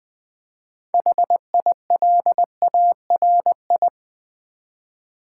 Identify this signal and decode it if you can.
Morse code, heard as a keyed tone, about 20 wpm, 700 Hz. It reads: HILARI
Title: Build A Word - 3 Letters At A Time - 9-Letter Words - 20wpm